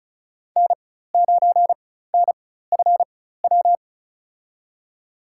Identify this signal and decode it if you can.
Morse code, heard as a keyed tone, about 36 wpm, spaced as for 16 wpm, 700 Hz. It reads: N9NFW
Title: USA Call Signs - 5 Characters - 4x Character Spacing - 35wpm